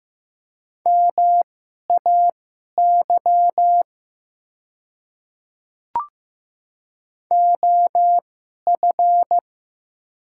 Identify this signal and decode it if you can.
Morse code, heard as a keyed tone, about 15 wpm, spaced as for 11 wpm, 700 Hz. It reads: MAY OF
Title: 2x Character Spacing - Top 100 Words - 15wpm